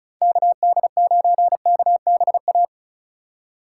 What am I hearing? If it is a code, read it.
Morse code, heard as a keyed tone, about 36 wpm, 700 Hz. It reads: KD9KBA
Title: USA Call Signs - 6 Characters - 8x Character Spacing - 35wpm